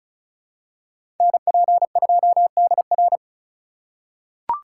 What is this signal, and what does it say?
Morse code, heard as a keyed tone, about 36 wpm, 700 Hz. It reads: NP2DR
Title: USA Call Signs - 5 Characters - 2x Character Spacing - 35wpm